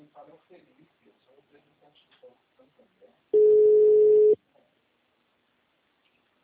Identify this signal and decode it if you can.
Ringback tone (Europe, 425 Hz), with speech.